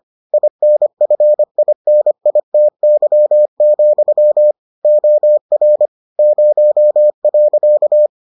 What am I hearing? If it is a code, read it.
Morse code, heard as a keyed tone, about 25 wpm, 600 Hz. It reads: INFINITY, OR 0.